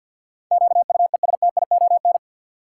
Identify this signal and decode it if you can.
Morse code, heard as a keyed tone, about 48 wpm, 700 Hz. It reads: QUESTION
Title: Sentences from Top 400 Words - New Word Drill - 50wpm